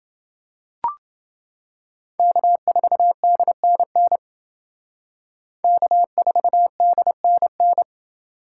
Morse code, heard as a keyed tone, about 28 wpm, 700 Hz. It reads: K4DNN K4DNN